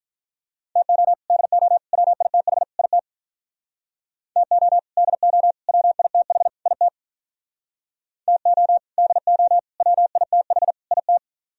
Morse code, heard as a keyed tone, about 43 wpm, 700 Hz. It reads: TO DO WITH IT TO DO WITH IT TO DO WITH IT
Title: Speed-Racing - Sets of 4 Words 35wpm